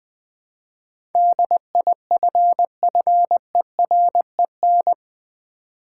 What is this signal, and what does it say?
Morse code, heard as a keyed tone, about 20 wpm, 700 Hz. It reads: DIFFEREN
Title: Build A Word - 4 Letters At A Time - 11-Letter Words - 20wpm